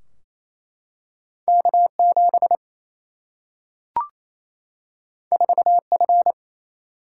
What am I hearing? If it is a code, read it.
Morse code, heard as a keyed tone, about 28 wpm, 700 Hz. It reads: K7 4F
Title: Single-Letter Single-Number 28wpm